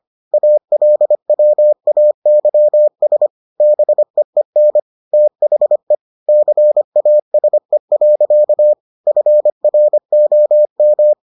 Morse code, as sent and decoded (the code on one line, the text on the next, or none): .- .-.. .-- .- -.-- ... / -... . -. / - .... / -.-. .- ... .-.-.- / ..-. .-. --- --
ALWAYS BEEN THE CASE. FROM